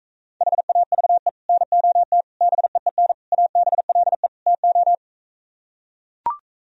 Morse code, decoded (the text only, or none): HAVE NOT BEEN ABLE TO